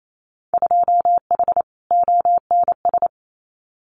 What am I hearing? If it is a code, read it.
Morse code, heard as a keyed tone, about 28 wpm, 700 Hz. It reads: 2H ONS
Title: Winter Field Day Exchanges Encore 28wpm